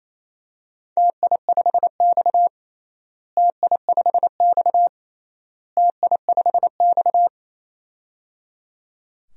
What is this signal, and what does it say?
Morse code, heard as a keyed tone, about 28 wpm, 700 Hz. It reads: TI5X TI5X TI5X
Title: DXCC-100 Call Signs - 4 Characters - Repeated 3x - 28wpm